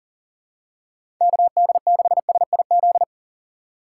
Morse code, heard as a keyed tone, about 40 wpm, 700 Hz. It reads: KD6SIZ
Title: USA Call Signs - 6 Characters - 4x Character Spacing - 40wpm